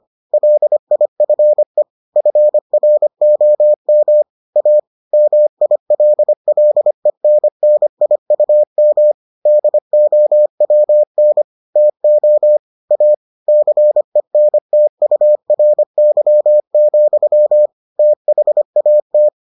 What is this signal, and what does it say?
Morse code, heard as a keyed tone, about 25 wpm, 600 Hz. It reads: LIFE FROM A MILLENNIUM DOWN TO A CENTURY, THAT